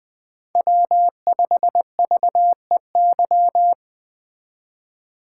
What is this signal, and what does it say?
Morse code, heard as a keyed tone, about 20 wpm, 700 Hz. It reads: W5VEY
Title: Speed Racing - USA 1x3 Call Signs 17wpm